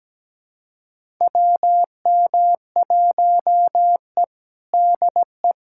Morse code, heard as a keyed tone, about 17 wpm, 700 Hz. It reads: WM1E DE
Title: USA Callsign DE Callsign - 4 Characters - 17wpm